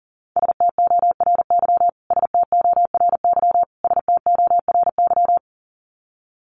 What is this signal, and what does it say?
Morse code, heard as a keyed tone, about 40 wpm, 700 Hz. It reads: STORY STORY STORY
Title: Top 500 Words Repeated 40wpm